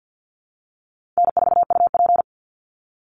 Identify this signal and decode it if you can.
Morse code, heard as a keyed tone, about 50 wpm, 700 Hz. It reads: N4UP
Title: Call Signs - 50wpm